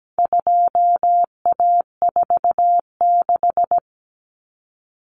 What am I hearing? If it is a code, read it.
Morse code, heard as a keyed tone, about 17 wpm, 700 Hz. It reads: 2A46
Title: Common Cut Numbers - 4-Digit 17wpm